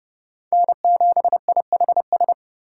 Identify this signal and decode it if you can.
Morse code, heard as a keyed tone, about 36 wpm, 700 Hz. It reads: N7IHS
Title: Speed Racing - USA 1x3 Call Signs 30wpm